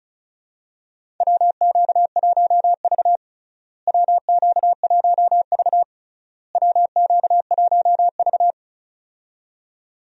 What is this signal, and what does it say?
Morse code, heard as a keyed tone, about 36 wpm, 700 Hz. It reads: WQ1V WQ1V WQ1V
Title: USA Call Signs - 4 Characters - Repeated 3x - 35wpm